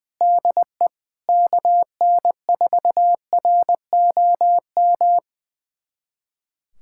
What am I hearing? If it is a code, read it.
Morse code, heard as a keyed tone, about 20 wpm, 700 Hz. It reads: DE KN4ROM
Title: USA Callsign DE Callsign 20wpm